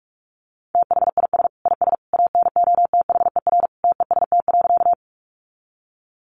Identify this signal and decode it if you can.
Morse code, heard as a keyed tone, about 44 wpm, 700 Hz. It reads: THIS IS ANOTHER TEST.
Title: Sentences from Top 600 Words 45wpm